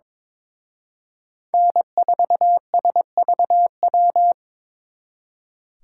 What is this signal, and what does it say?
Morse code, heard as a keyed tone, about 26 wpm, 700 Hz. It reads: N4SVW
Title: Speed Racing - USA 1x3 Call Signs 22wpm